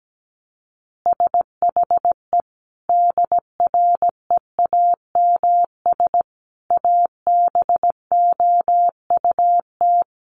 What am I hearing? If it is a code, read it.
Morse code, heard as a keyed tone, about 17 wpm, 700 Hz. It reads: SHE DREAMS ABOUT
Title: Easy Phrases 17wpm